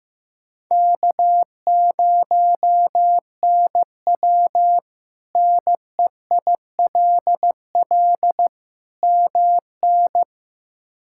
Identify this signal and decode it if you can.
Morse code, heard as a keyed tone, about 16 wpm, 700 Hz. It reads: K0NW NEILL MN
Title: K1USN SST QSO Practice - 15wpm